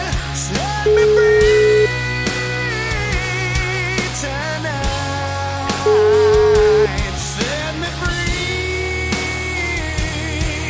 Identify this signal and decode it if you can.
Ringback tone (Europe, 425 Hz), with noise.